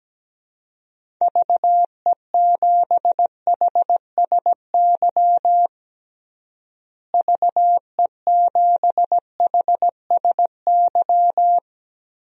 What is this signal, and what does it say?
Morse code, heard as a keyed tone, about 17 wpm, 700 Hz. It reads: VE7HSY VE7HSY
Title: DXCC-100 Call Signs - 6 Characters - Repeated 2x - 17wpm